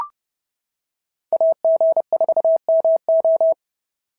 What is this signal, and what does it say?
Morse code, heard as a keyed tone, about 29 wpm, 690 Hz. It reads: AG4MO
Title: SOTA QSO Practice 30wpm